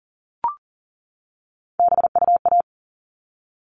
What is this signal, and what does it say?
Morse code, heard as a keyed tone, about 40 wpm, 700 Hz. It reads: BUA
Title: All Cut Numbers - 3-Digit 40wpm